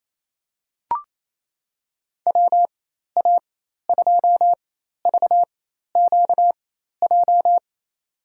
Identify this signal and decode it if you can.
Morse code, heard as a keyed tone, about 27 wpm, spaced as for 13 wpm, 700 Hz. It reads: WA2VQJ
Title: USA Call Signs - 6 Characters - 4x Character Spacing - 28wpm